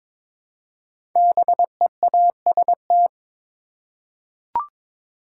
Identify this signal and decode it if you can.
Morse code, heard as a keyed tone, about 22 wpm, 700 Hz. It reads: BEAST
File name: 5 Letter Words - First 3 Letters Given - 22wpm